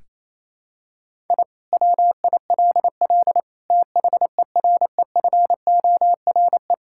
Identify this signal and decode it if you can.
Morse code, heard as a keyed tone, about 28 wpm, 700 Hz. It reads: I WILL THEREFORE
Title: Sentences from Top 1000 Words - Review Part 5 - 28wpm